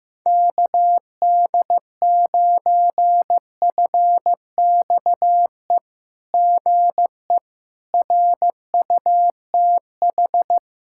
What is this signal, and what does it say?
Morse code, heard as a keyed tone, about 15 wpm, 700 Hz. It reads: KD9FXE GE RUTH
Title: SOTA QSO Practice 15wpm